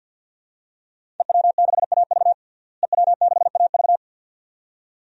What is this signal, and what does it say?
Morse code, heard as a keyed tone, about 52 wpm, 700 Hz. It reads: EW6AV EW6AV
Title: DXCC-100 Call Signs - 5 Characters - Repeated 2x - 50wpm